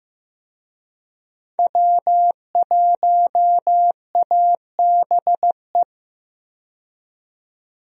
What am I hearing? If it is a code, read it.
Morse code, heard as a keyed tone, about 15 wpm, 700 Hz. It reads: W1ABE
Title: USA Call Signs - 5 Characters - 15wpm